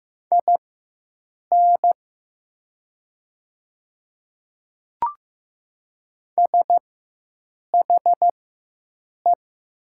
Morse code, heard as a keyed tone, about 15 wpm, spaced as for 7 wpm, 700 Hz. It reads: IN SHE
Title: 4x Character Spacing - Top 100 Words - 15wpm